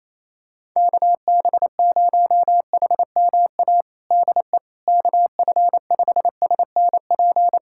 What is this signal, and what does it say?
Morse code, heard as a keyed tone, about 28 wpm, 700 Hz. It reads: KB0HMA DE KF5SNP
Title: USA Callsign DE Callsign - 6 Characters - 28wpm